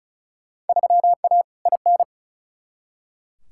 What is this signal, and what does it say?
Morse code, heard as a keyed tone, about 36 wpm, 700 Hz. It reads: 3A IN
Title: Field Day Exchanges - 35wpm